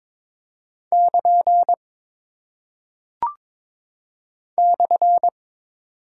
Morse code, heard as a keyed tone, about 22 wpm, 700 Hz. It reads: <KN> /
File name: practice / 22wpm / Single Punctuation-Prosign - 22wpm